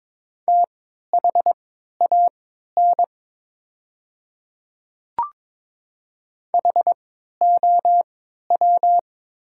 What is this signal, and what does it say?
Morse code, heard as a keyed tone, about 22 wpm, spaced as for 13 wpm, 690 Hz. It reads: THAN HOW
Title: 3x Character Spacing - Top 200 Words - 22wpm